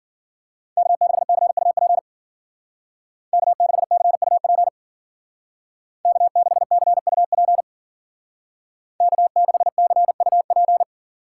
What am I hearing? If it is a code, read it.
Morse code, heard as a keyed tone, about 49 wpm, 700 Hz. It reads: K6CUP K6CUP K6CUP K6CUP
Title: Speed Racing - USA 1x3 Call Signs 40wpm